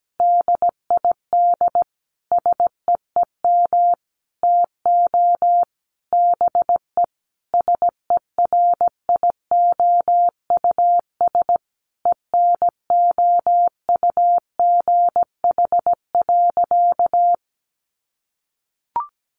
Morse code, heard as a keyed tone, about 17 wpm, 700 Hz. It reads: DID SEEM TO BE SERIOUS ENOUGH.
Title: Sentences from Top 1100 Words 17wpm